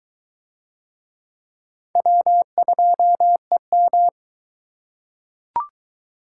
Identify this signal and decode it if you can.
Morse code, heard as a keyed tone, about 22 wpm, 700 Hz. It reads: W2EM